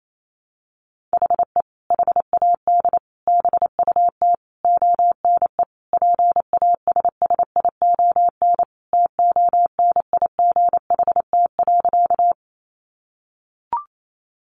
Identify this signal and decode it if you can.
Morse code, heard as a keyed tone, about 28 wpm, 700 Hz. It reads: HE HAD BUT ONE PASSION TONIGHT.
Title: Sentences from Top 1300 Words 28wpm